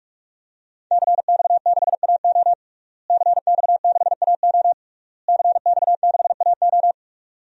Morse code, heard as a keyed tone, about 42 wpm, 700 Hz. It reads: CX6AO CX6AO CX6AO
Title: DXCC-100 Call Signs - 5 Characters - Repeated 3x - 45wpm